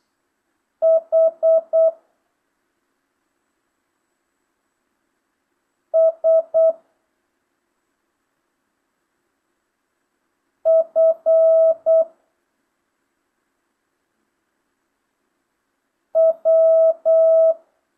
Morse code, heard as a keyed tone, about 8 wpm, 650 Hz. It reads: H S F W
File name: jh8cbh / 2025cw